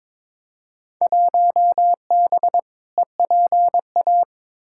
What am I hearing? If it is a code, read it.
Morse code, heard as a keyed tone, about 22 wpm, 700 Hz. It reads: 1B EPA
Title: Field Day Exchanges - 22wpm